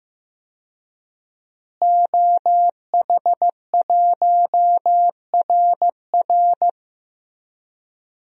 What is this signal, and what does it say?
Morse code, heard as a keyed tone, about 15 wpm, 700 Hz. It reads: OH1RR
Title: DXCC-100 Call Signs - 5 Characters - Repeated 3x - 15wpm